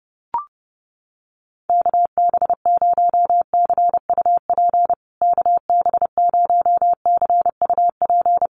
Morse code, heard as a keyed tone, about 30 wpm, 700 Hz. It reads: KB0CUP KB0CUP
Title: USA Call Signs Repeated 3x 30wpm